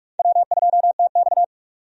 Morse code, heard as a keyed tone, about 46 wpm, 700 Hz. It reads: W2TX
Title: USA Call Signs - 4 Characters - 45wpm